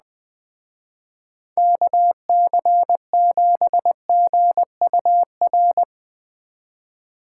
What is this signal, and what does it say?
Morse code, heard as a keyed tone, about 20 wpm, 700 Hz. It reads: KC7GUR
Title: Speed Racing - USA 2x3 Call Signs 17wpm